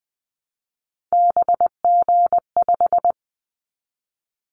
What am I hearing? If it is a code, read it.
Morse code, heard as a keyed tone, about 19 wpm, 700 Hz. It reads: BG5
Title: Callsign Prefixes for Top 100 DXCC - 20wpm